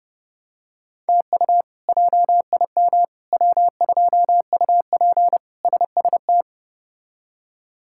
Morse code, heard as a keyed tone, about 30 wpm, 700 Hz. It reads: TU JIM W2UP SST
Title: K1USN SST QSO Practice - 30wpm